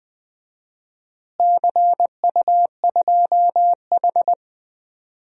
Morse code, heard as a keyed tone, about 20 wpm, 700 Hz. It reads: CU2H
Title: DXCC-100 Call Signs - 4 Characters - Repeated 3x - 20wpm